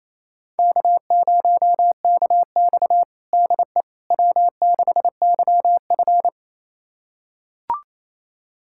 Morse code, heard as a keyed tone, about 28 wpm, 700 Hz. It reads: K0KX DE W6YF
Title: USA Callsign DE Callsign - 4 Characters - 28wpm